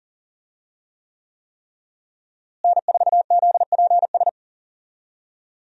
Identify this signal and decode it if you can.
Morse code, heard as a keyed tone, about 40 wpm, 700 Hz. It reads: N4ZPS
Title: USA Call Signs - 5 Characters - Repeated 3x - 40wpm